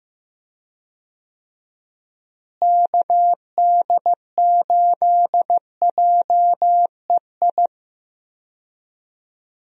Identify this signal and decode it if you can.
Morse code, heard as a keyed tone, about 15 wpm, 700 Hz. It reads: KD8JEI
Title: USA Call Signs - 6 Characters - Repeated 3x - 15wpm